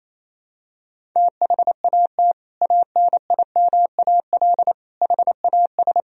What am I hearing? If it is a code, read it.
Morse code, heard as a keyed tone, about 28 wpm, 700 Hz. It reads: THAT ANIMAL HAS
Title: Sentences from Top 1000 Words - Review Part 1 - 28wpm